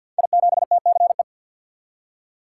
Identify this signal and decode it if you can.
Morse code, heard as a keyed tone, about 52 wpm, 700 Hz. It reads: I7TCE
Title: Code Groups - 5 Letter:Numbers - 50wpm